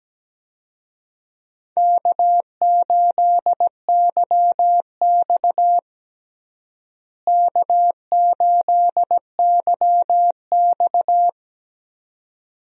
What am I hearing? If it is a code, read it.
Morse code, heard as a keyed tone, about 17 wpm, 700 Hz. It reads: K8YX K8YX
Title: USA Call Signs - 4 Characters - Repeated 2x - 17wpm